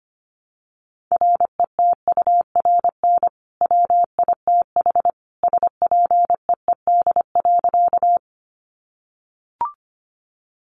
Morse code, heard as a keyed tone, about 25 wpm, 700 Hz. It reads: RETURN WITH SPEED.
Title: Sentences from Top 800 Words 25wpm